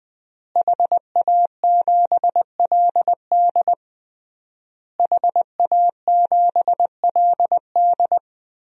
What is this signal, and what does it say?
Morse code, heard as a keyed tone, about 20 wpm, 700 Hz. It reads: HA7LD HA7LD
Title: DXCC-100 Call Signs - 5 Characters - Repeated 3x - 20wpm